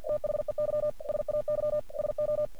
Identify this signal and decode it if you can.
Morse code, heard as a keyed tone, about 46 wpm, 600 Hz. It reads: THEY SAY SO
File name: archived / CRSnet